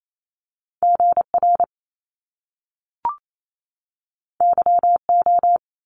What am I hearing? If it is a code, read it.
Morse code, heard as a keyed tone, about 28 wpm, 700 Hz. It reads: GR YO
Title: Top Two Letter Combinations in Words 28wpm